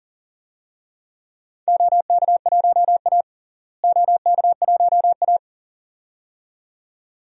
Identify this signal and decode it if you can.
Morse code, heard as a keyed tone, about 40 wpm, 700 Hz. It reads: OK1A OK1A
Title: DXCC-100 Call Signs - 4 Characters - Repeated 2x - 40wpm